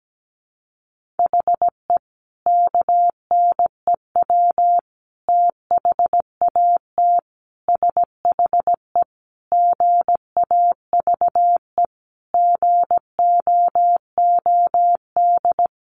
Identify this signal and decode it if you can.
Morse code, heard as a keyed tone, about 17 wpm, 700 Hz. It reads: HE KNEW THAT SHE GAVE GOOD
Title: Sentences from Top 1100 Words 17wpm